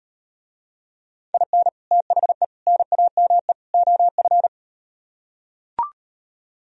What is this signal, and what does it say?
Morse code, heard as a keyed tone, about 38 wpm, 700 Hz. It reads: IN THE NAME OF